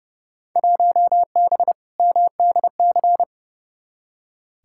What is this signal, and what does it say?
Morse code, heard as a keyed tone, about 31 wpm, 700 Hz. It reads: 1B MDC
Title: Field Day Exchanges - 30wpm